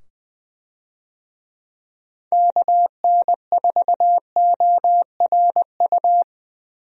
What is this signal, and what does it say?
Morse code, heard as a keyed tone, about 20 wpm, 700 Hz. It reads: KN4ORU
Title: USA Call Signs - 6 Characters - 20wpm